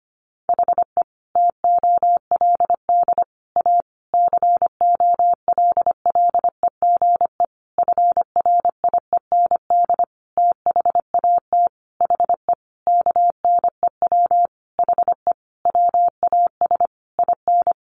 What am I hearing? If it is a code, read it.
Morse code, heard as a keyed tone, about 25 wpm, 700 Hz. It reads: HE TOLD A COLLEGE FRIEND THAT HE KNEW HE WAS IN
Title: Sentences from Top 1300 Words 25wpm